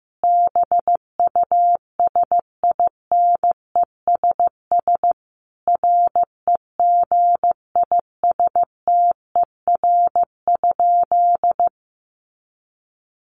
Morse code, heard as a keyed tone, about 15 wpm, 700 Hz. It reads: BUSINESS REGISTER?
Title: Sentences from Top 1200 Words 15wpm